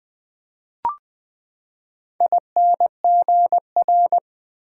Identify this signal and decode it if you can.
Morse code, heard as a keyed tone, about 20 wpm, 700 Hz. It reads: INGR
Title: Build A Word - 4 Letters At A Time - 10-Letter Words - 20wpm